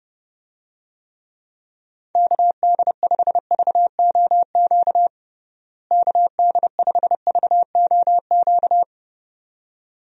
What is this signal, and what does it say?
Morse code, heard as a keyed tone, about 30 wpm, 700 Hz. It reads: KD5VOQ KD5VOQ
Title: USA Call Signs - 6 Characters - Repeated 2x - 30wpm